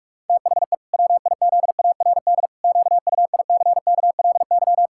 Morse code, heard as a keyed tone, about 46 wpm, 700 Hz. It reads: THE WIZARD QUICKLY
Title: Pangram Challenge v2 - 45wpm